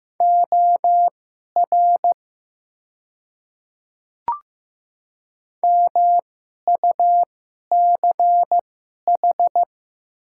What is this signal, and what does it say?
Morse code, heard as a keyed tone, about 15 wpm, spaced as for 11 wpm, 700 Hz. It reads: OR MUCH
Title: 2x Character Spacing - Top 300 Words - 15wpm